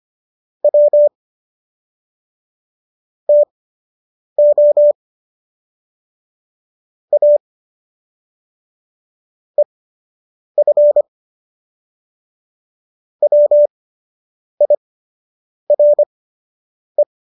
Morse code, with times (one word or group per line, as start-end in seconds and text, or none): 0.6-1.1 W
3.3-4.9 TO
7.1-7.4 A
9.6-11.0 EF
13.2-17.0 WIRE